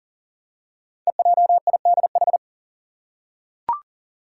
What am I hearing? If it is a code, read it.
Morse code, heard as a keyed tone, about 39 wpm, 700 Hz. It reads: EJIDH
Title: Code Groups - 5 Letters - 40wpm